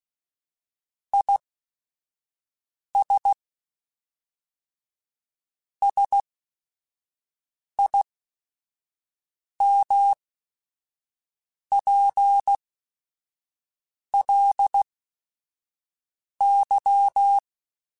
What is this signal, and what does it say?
Morse code, heard as a keyed tone, about 16 wpm, spaced as for 5 wpm, 800 Hz. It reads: IS SIMPLY
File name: Technical Subjects / Technical Subjects-mp3s / XL @5